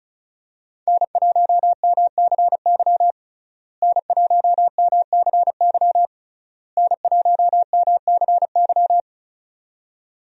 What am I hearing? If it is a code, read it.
Morse code, heard as a keyed tone, about 34 wpm, 700 Hz. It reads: N1MCY N1MCY N1MCY
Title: USA Call Signs - 5 Characters - Repeated 3x - 35wpm